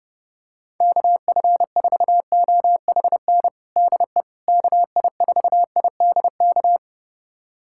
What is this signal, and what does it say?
Morse code, heard as a keyed tone, about 32 wpm, 700 Hz. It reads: KF4OHN DE KI4IDK
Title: USA Callsign DE Callsign - 6 Characters - 30wpm